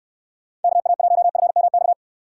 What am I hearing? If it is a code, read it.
Morse code, heard as a keyed tone, about 69 wpm, 700 Hz. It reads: KA1FMX